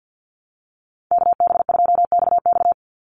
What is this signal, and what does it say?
Morse code, heard as a keyed tone, about 50 wpm, 700 Hz. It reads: KB2XX
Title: NA SOTA Chaser Callsigns 50wpm